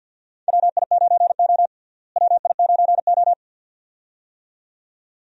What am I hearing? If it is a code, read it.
Morse code, heard as a keyed tone, about 52 wpm, 700 Hz. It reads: WI9O WI9O
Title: USA Call Signs - 4 Characters - Repeated 2x - 50wpm